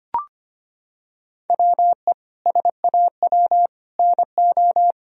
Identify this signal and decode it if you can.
Morse code, heard as a keyed tone, about 25 wpm, 700 Hz. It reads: WE SAW NO
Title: Sentences from Top 1000 Words - Review Part 1 - 25wpm